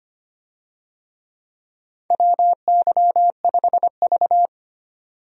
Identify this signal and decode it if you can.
Morse code, heard as a keyed tone, about 25 wpm, 700 Hz. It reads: WY5V
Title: USA Call Signs - 4 Characters - 25wpm